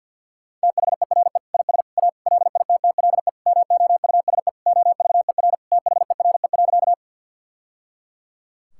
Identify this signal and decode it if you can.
Morse code, heard as a keyed tone, about 46 wpm, 700 Hz. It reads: THERE IS A LITTLE MOUSE OVER THERE.
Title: Sentences from Top 1000 Words - Review Part 5 - 50wpm